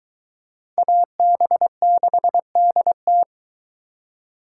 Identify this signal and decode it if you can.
Morse code, heard as a keyed tone, about 23 wpm, 700 Hz. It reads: AB6DT